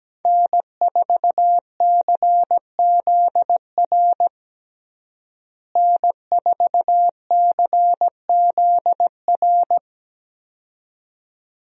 Morse code, heard as a keyed tone, about 17 wpm, 700 Hz. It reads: N4CZR N4CZR
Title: USA Call Signs - 5 Characters - Repeated 2x - 17wpm